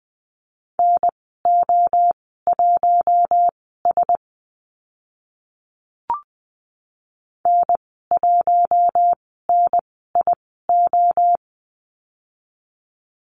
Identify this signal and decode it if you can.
Morse code, heard as a keyed tone, about 20 wpm, spaced as for 14 wpm, 700 Hz. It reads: NO1S N1NIO
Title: 042 Lesson 8g - Number 1 - Call Signs 20wpm